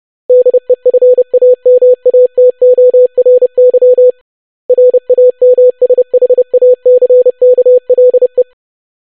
Morse code, heard as a keyed tone, about 30 wpm, 500 Hz. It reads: DEFAMATORY RAMSHACKLE
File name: longwords-25wpm